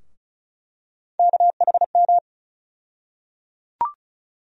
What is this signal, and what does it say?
Morse code, heard as a keyed tone, about 36 wpm, 700 Hz. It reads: KHM
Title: USA Call Sign Postfixes - 3 Letters 35wpm